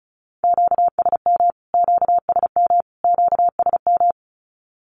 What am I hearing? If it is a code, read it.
Morse code, heard as a keyed tone, about 34 wpm, 700 Hz. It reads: QSM QSM QSM
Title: Q-Codes Repeated 3x 35wpm